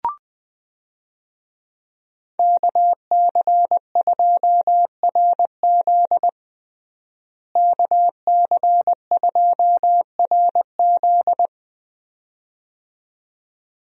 Morse code, heard as a keyed tone, about 20 wpm, 700 Hz. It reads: KC2RZ KC2RZ
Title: USA Call Signs - 5 Characters - Repeated 2x - 20wpm